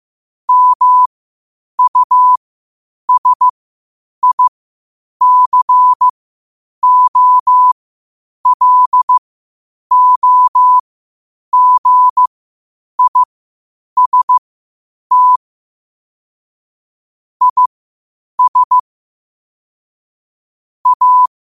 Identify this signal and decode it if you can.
Morse code, heard as a keyed tone, about 15 wpm, spaced as for 8 wpm, 1000 Hz. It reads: MUSICOLOGIST IS A
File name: 8wpm